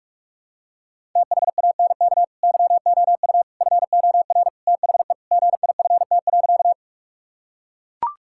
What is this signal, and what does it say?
Morse code, heard as a keyed tone, about 41 wpm, 700 Hz. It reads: THANK YOU FOR THE GIFT.
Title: Sentences from Top 1000 Words - Review Part 6 - 45wpm